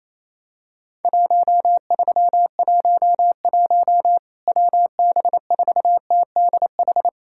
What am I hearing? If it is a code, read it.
Morse code, heard as a keyed tone, about 28 wpm, 700 Hz. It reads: 1311 WB4TDH
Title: Worked All Europe DX Contest - QTC Exchange - 28wpm